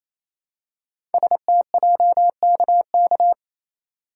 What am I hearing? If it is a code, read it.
Morse code, heard as a keyed tone, about 28 wpm, 700 Hz. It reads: STJKK
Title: Code Groups - 5 Letters - 28wpm